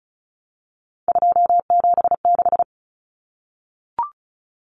Morse code, heard as a keyed tone, about 34 wpm, 700 Hz. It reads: 276
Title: Numbers - 3-Digit 35wpm